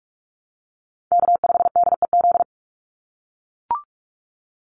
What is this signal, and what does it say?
Morse code, heard as a keyed tone, about 44 wpm, 700 Hz. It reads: K5DEZ
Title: NA SOTA Chaser Callsigns 45wpm